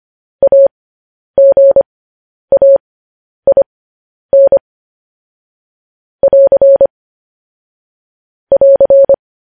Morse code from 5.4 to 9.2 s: <AR> <AR>

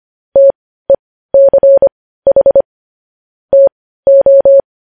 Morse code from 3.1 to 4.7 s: TO

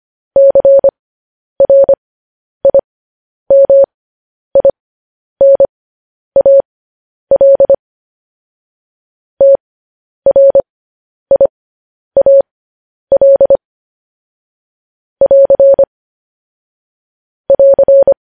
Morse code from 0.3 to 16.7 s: CRIMINAL TRIAL <AR>

